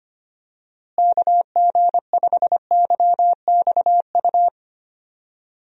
Morse code, heard as a keyed tone, about 25 wpm, 700 Hz. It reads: KG5YXU
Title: USA Call Signs - 6 Characters - Repeated 3x - 25wpm